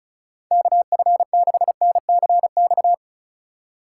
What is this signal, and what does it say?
Morse code, heard as a keyed tone, about 34 wpm, 700 Hz. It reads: KF6NCX
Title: K1USN SST QSO Practice - 35wpm